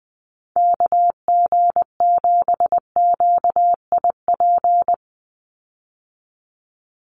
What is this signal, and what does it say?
Morse code, heard as a keyed tone, about 20 wpm, 700 Hz. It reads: KG7QIP
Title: USA Call Signs 20wpm